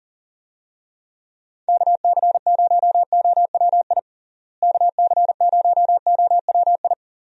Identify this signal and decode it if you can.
Morse code, heard as a keyed tone, about 40 wpm, 700 Hz. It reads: KC0OWI KC0OWI
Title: USA Call Signs - 6 Characters - Repeated 3x - 40wpm